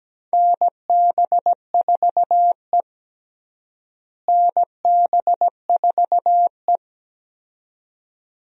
Morse code, heard as a keyed tone, about 17 wpm, 700 Hz. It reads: NB4E NB4E
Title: USA Call Signs - 4 Characters - Repeated 2x - 17wpm